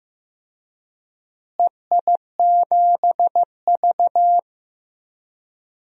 Morse code, heard as a keyed tone, about 15 wpm, 700 Hz. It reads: EI7V
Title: DXCC-100 Call Signs - 4 Characters - Repeated 2x - 15wpm